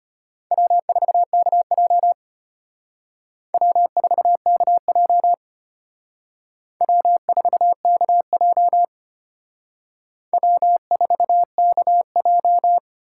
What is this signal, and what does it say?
Morse code, heard as a keyed tone, about 31 wpm, 700 Hz. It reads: W4KJ W4KJ W4KJ W4KJ